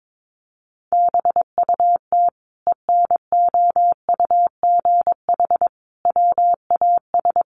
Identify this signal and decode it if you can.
Morse code, heard as a keyed tone, about 22 wpm, 700 Hz. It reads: BUT ENOUGH WAS
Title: Sentences from Top 1100 Words 22wpm